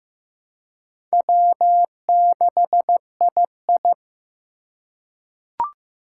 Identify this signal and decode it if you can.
Morse code, heard as a keyed tone, about 15 wpm, 700 Hz. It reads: W6II
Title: USA Call Signs - 3x Character Spacing - 15wpm